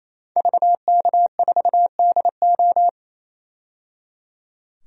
Morse code, heard as a keyed tone, about 28 wpm, 700 Hz. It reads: VK4DO
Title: DXCC-100 Call Signs - 5 Characters - 28wpm